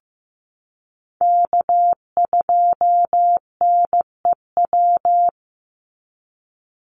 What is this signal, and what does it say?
Morse code, heard as a keyed tone, about 15 wpm, 700 Hz. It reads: K2NEW
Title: USA Call Signs 15wpm